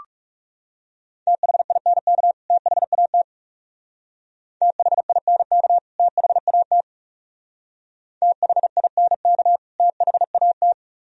Morse code, heard as a keyed tone, about 37 wpm, 700 Hz. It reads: THINK THAT THINK THAT THINK THAT
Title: Speed-Racing - Sets of 2 Words 30wpm